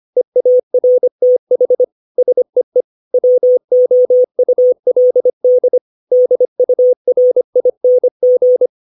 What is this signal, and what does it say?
Morse code, heard as a keyed tone, about 25 wpm, 500 Hz. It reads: EARTH SEE WOULD DURING
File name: 632 words / 632 Words 25WPM